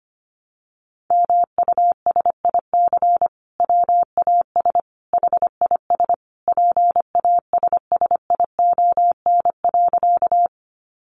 Morse code, heard as a keyed tone, about 25 wpm, 700 Hz. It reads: MUSIC WAS HIS PASSION.
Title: Sentences from Top 1300 Words 25wpm